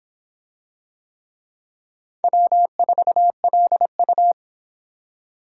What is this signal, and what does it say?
Morse code, heard as a keyed tone, about 23 wpm, 700 Hz. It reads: W4LU